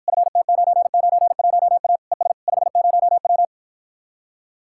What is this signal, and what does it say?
Morse code, heard as a keyed tone, about 53 wpm, 700 Hz. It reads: FT991A ES 50W